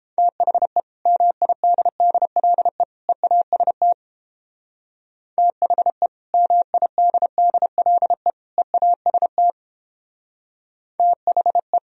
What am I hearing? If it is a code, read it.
Morse code, heard as a keyed tone, about 30 wpm, 700 Hz. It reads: THE MIDDLE EAST THE MIDDLE EAST THE